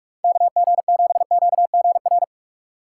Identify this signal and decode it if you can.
Morse code, heard as a keyed tone, about 46 wpm, 700 Hz. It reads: KG7QGR
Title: USA Call Signs - 6 Characters - 6x Character Spacing - 45wpm